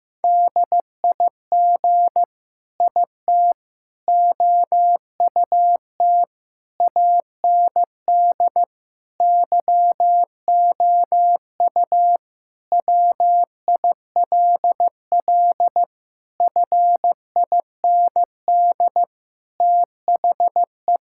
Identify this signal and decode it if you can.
Morse code, heard as a keyed tone, about 15 wpm, 700 Hz. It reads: DIG IT OUT AND YOU WILL FIND THE
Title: Sentences from Top 1600 Words 15wpm